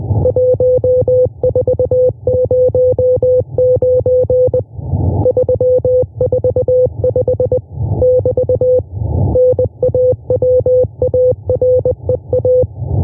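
Morse code, heard as a keyed tone, about 20 wpm, 500 Hz. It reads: 1409 345 <BT> NAWAREA
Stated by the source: A1A Morse telegraphy transmission by the "RCV" station, an intruder heard in the HF amateur bands, monitored by the IARU.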